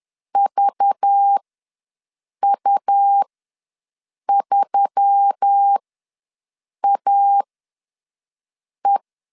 Morse code, heard as a keyed tone, about 10 wpm, spaced as for 6 wpm, 800 Hz. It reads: VU3AE
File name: LEZIONI / 5 LBVC / 5 LBV ws 10wpm 3645